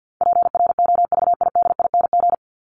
Morse code, heard as a keyed tone, about 50 wpm, 700 Hz. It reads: PROVIDING
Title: ING Suffix - Top 100 Words with Base Words 50wpm